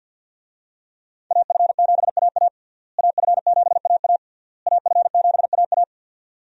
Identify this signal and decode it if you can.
Morse code, heard as a keyed tone, about 51 wpm, 700 Hz. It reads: AF7AA AF7AA AF7AA
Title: USA Call Signs - 5 Characters - Repeated 3x - 50wpm